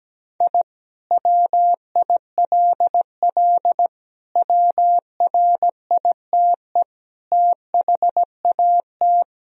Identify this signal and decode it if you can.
Morse code, heard as a keyed tone, about 17 wpm, 700 Hz. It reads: I WILL WRITE THAT
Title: Sentences from Top 1000 Words - Review Part 3 - 17wpm